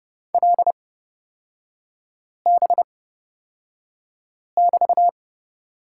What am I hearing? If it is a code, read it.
Morse code, heard as a keyed tone, about 29 wpm, spaced as for 5 wpm, 700 Hz. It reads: LB<BT>
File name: Lesson 36e - BT - Three Characters 30wpm